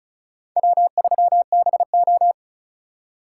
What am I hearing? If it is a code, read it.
Morse code, heard as a keyed tone, about 43 wpm, 700 Hz. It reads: W3BO
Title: Speed Racing - USA Call Signs 35wpm